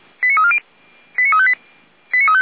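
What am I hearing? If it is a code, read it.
Five-tone selective call (ZVEI), heard as quick runs of five tones, tones of about 70 ms. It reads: 89340 89358
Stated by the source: ZVEI five-tone selective call received on an SDR.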